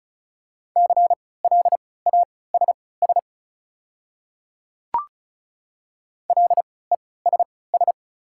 Morse code, heard as a keyed tone, about 36 wpm, spaced as for 20 wpm, 700 Hz. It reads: CLASS LESS
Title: 3x Character Spacing - Top 500 Words - 35wpm